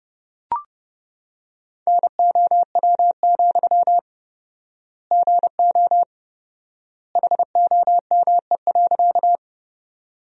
Morse code, heard as a keyed tone, about 30 wpm, 700 Hz. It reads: NOW, GO HOME.
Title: 4x Word-Spacing - Sentences from Top 200 Words - 30wpm